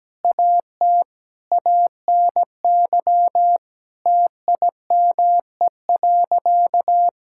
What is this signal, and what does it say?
Morse code, heard as a keyed tone, about 17 wpm, 700 Hz. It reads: AT ANY TIME.
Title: Sentences from Top 1000 Words - Review Part 6 - 17wpm